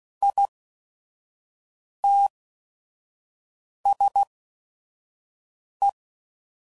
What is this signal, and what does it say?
Morse code, heard as a keyed tone, about 16 wpm, spaced as for 5 wpm, 800 Hz. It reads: ITS E
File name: CW Code Groups @5WPM / Technical Subjects / XC